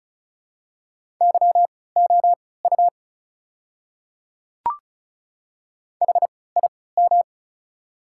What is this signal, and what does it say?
Morse code, heard as a keyed tone, about 36 wpm, spaced as for 20 wpm, 690 Hz. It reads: YOU HIM
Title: 3x Character Spacing - Top 100 Words - 35wpm